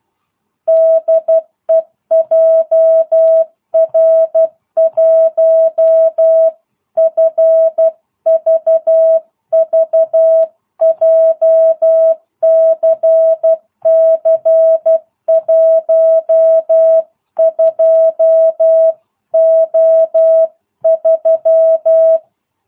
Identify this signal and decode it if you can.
Morse code, heard as a keyed tone, about 12 wpm, 660 Hz. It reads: DE JR1FVV JCC 12O3